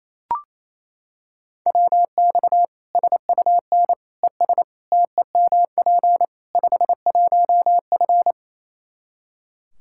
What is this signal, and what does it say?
Morse code, heard as a keyed tone, about 28 wpm, 700 Hz. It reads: WX SUN ES TEMP 51F
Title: Practice QSO - Exchange 2 - Info - 28wpm